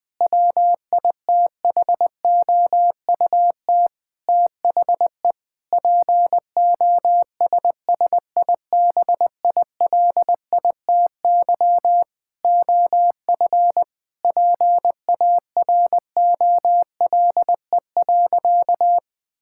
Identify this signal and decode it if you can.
Morse code, heard as a keyed tone, about 20 wpm, 700 Hz. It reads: WITHOUT THE POSSIBILITY OF PAROLE.
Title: Sentences with 8-Word Phrases - 20wpm